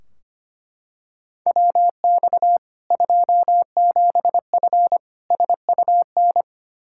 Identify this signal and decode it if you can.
Morse code, heard as a keyed tone, about 25 wpm, 700 Hz. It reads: WX 27F SUN
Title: Practice QSO - Exchange 2 - Info - 25wpm